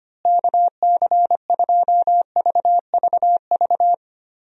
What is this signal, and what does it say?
Morse code, heard as a keyed tone, about 25 wpm, 700 Hz. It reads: KC2VVV
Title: USA Call Signs - 6 Characters - Repeated 2x - 25wpm